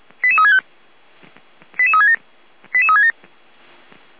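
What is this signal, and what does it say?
Five-tone selective call (ZVEI), heard as quick runs of five tones, tones of about 70 ms. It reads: 88365 88367 88367
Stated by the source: ZVEI five-tone selective call received on an SDR.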